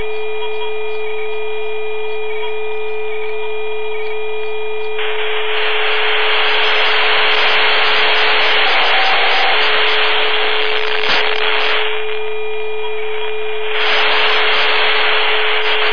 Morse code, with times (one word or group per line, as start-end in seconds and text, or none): none